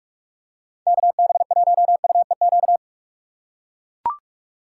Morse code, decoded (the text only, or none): KB1UEQ